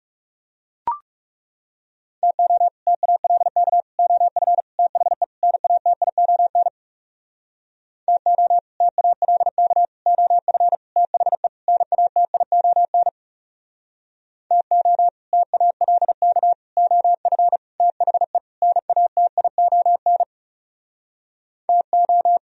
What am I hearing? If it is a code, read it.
Morse code, heard as a keyed tone, about 40 wpm, 700 Hz. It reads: TO TALK OF THE NATION TO TALK OF THE NATION TO TALK OF THE NATION TO